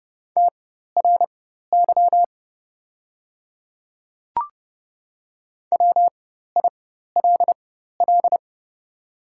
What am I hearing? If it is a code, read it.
Morse code, heard as a keyed tone, about 30 wpm, spaced as for 14 wpm, 700 Hz. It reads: TRY WILL
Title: 4x Character Spacing - Top 300 Words - 30wpm